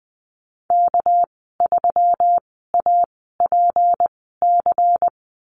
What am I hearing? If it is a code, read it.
Morse code, heard as a keyed tone, about 20 wpm, spaced as for 15 wpm, 700 Hz. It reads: K3APC
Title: 219 Lesson 35g - Number 0 - Call Signs 20wpm